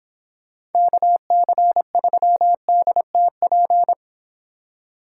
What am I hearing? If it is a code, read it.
Morse code, heard as a keyed tone, about 29 wpm, 700 Hz. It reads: KC3DTP